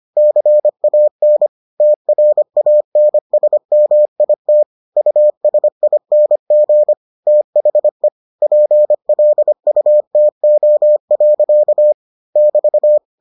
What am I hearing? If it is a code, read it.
Morse code, heard as a keyed tone, about 25 wpm, 600 Hz. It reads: CAN TRANSMIT USING THE PLUTO. <BT>